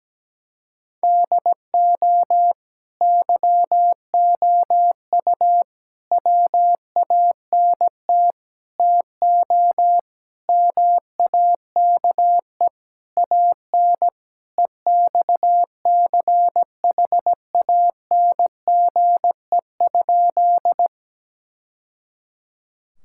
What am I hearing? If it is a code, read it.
Morse code, heard as a keyed tone, about 17 wpm, 700 Hz. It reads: DO YOU WANT TO MAKE AN EXCHANGE?
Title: Sentences from Top 1000 Words - Review Part 6 - 17wpm